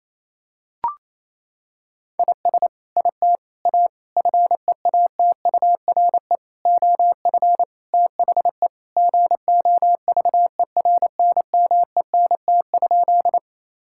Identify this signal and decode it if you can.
Morse code, heard as a keyed tone, about 28 wpm, 700 Hz. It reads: IS IT A FEATURE OF THE GOVERNMENT?
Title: Sentences from Top 1000 Words - Review Part 5 - 28wpm